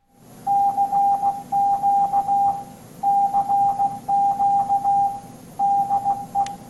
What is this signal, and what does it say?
Morse code, heard as a keyed tone, about 16 wpm, 800 Hz. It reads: CQ CQ DE